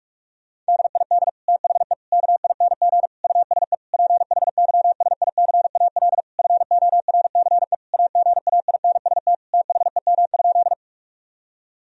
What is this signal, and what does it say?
Morse code, heard as a keyed tone, about 45 wpm, 700 Hz. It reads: DID THE KING USE PHYSICAL FORCE AGAINST THEM?
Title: Sentences from Top 1000 Words - Review Part 4 - 45wpm